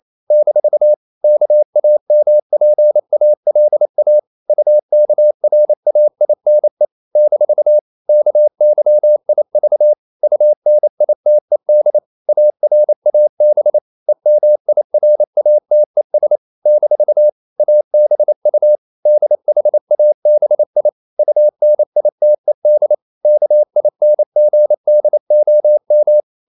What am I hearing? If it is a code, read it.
Morse code, heard as a keyed tone, about 28 wpm, 600 Hz. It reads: - KAMPALA UKRAINE - KYIV UNITED ARAB EMIRATES - ABU DHABI UNITED KINGDOM
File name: CW / Practice / NationalCapitals28